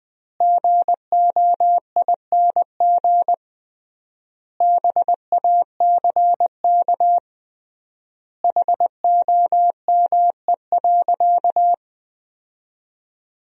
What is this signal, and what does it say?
Morse code, heard as a keyed tone, about 20 wpm, 700 Hz. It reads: GOING BACK HOME.
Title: 3x Word-Spacing - Sentences from Top 500 Words - 20wpm